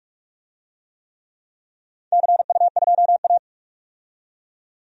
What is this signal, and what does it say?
Morse code, heard as a keyed tone, about 46 wpm, 700 Hz. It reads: CU2A
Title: DXCC-100 Call Signs - 4 Characters - 45wpm